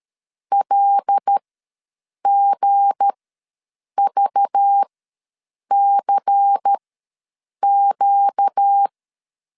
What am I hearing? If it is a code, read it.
Morse code, heard as a keyed tone, about 13 wpm, spaced as for 7 wpm, 800 Hz. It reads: LGVCQ
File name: LEZIONI / 7 GF9Z / 7 G ws 12wpm 3140